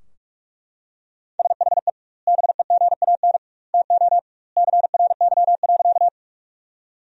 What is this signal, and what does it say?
Morse code, heard as a keyed tone, about 41 wpm, 700 Hz. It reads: SHE BEGAN TO CRY.
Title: 2x Word-Spacing - Sentences from Top 500 Words - 45wpm